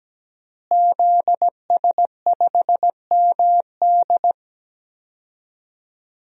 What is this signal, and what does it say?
Morse code, heard as a keyed tone, about 17 wpm, 700 Hz. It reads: ZS5MD
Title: DXCC-100 Call Signs - 5 Characters - Repeated 2x - 17wpm